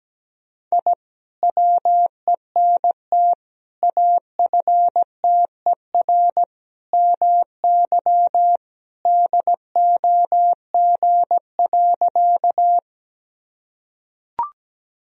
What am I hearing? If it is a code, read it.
Morse code, heard as a keyed tone, about 17 wpm, 700 Hz. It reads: I WENT AFTER MY DOG.
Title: Sentences from Top 1000 Words - Review Part 2 - 17wpm